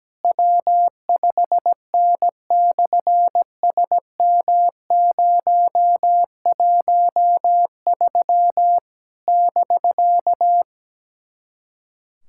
Morse code, heard as a keyed tone, about 17 wpm, 700 Hz. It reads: W5N/SM013 <BK>
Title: SOTA QSO Practice 17wpm